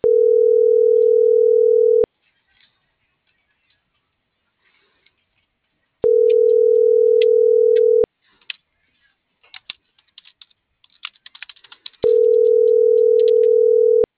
Ringback tone (North America), 440 + 480 Hz.